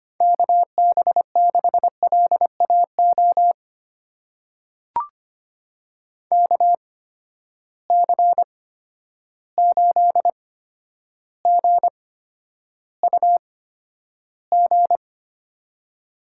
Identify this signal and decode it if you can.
Morse code, heard as a keyed tone, about 25 wpm, spaced as for 11 wpm, 700 Hz. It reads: KB6LAO KC8GUG
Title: USA Call Signs - 6 Characters - 8x Character Spacing - 25wpm